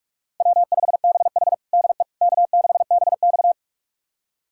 Morse code, heard as a keyed tone, about 46 wpm, 700 Hz. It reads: W5BH DE K6BX
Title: USA Callsign DE Callsign - 4 Characters - 45wpm